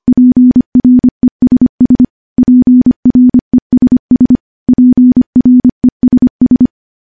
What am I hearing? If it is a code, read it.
Morse code, heard as a keyed tone, about 25 wpm, 250 Hz. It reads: PRESS PRESS PRESS
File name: morse-training-25WPM